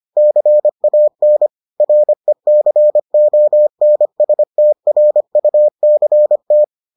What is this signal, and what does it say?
Morse code, heard as a keyed tone, about 25 wpm, 600 Hz. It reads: CAN RECONSTRUCT